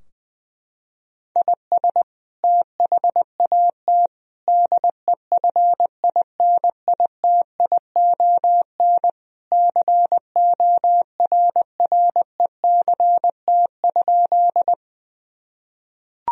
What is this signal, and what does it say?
Morse code, heard as a keyed tone, about 20 wpm, 700 Hz. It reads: IS THAT DEFINITION CORRECT?
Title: Sentences from Top 1000 Words - Review Part 4 - 20wpm